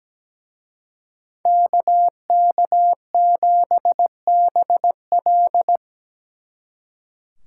Morse code, heard as a keyed tone, about 20 wpm, 700 Hz. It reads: KK7BL
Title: Speed Racing - USA 2x2 Call Signs 17wpm